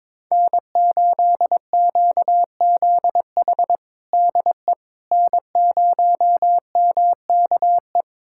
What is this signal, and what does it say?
Morse code, heard as a keyed tone, about 22 wpm, 700 Hz. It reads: N8QZH DE N0MKE
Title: USA Callsign DE Callsign - 5 Characters - 22wpm